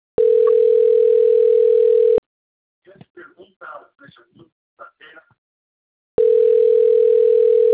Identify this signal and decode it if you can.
Ringback tone (North America), 440 + 480 Hz, with speech.
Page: Limeira V